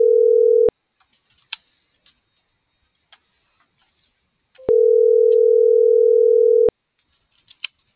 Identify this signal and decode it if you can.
Ringback tone (North America), 440 + 480 Hz.